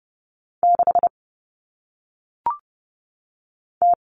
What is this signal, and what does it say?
Morse code, heard as a keyed tone, about 30 wpm, 700 Hz. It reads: 6 T
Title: All Cut Numbers - 1-Digit 30wpm